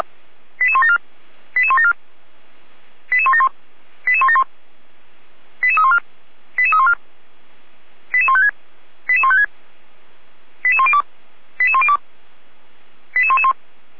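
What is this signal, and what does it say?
Five-tone selective call (ZVEI), heard as quick runs of five tones, tones of about 70 ms. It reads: 88174 88174 88171 88171 88314 88314 88156 88156 88102 88102 88111
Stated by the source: ZVEI five-tone selective call received on an SDR, archived by Feuerwehr Orlamünde fire brigade.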